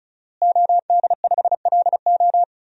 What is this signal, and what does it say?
Morse code, heard as a keyed tone, about 34 wpm, 700 Hz. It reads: OD5LO
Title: DXCC-100 Call Signs - 5 Characters - Repeated 3x - 35wpm